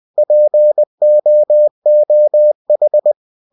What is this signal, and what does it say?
Morse code, heard as a keyed tone, about 20 wpm, 600 Hz. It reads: POOH
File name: cw-books / winnie-the-pooh / winnie-20wpm-page-0030